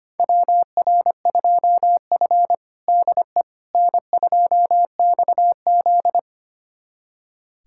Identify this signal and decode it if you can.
Morse code, heard as a keyed tone, about 25 wpm, 700 Hz. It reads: WR2F DE N2XZ
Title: USA Callsign DE Callsign - 4 Characters - 25wpm